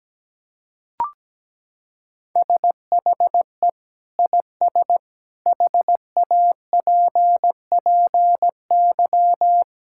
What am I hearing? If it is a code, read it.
Morse code, heard as a keyed tone, about 17 wpm, 700 Hz. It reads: SHE IS HAPPY
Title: Sentences from Top 1000 Words - Review Part 6 - 17wpm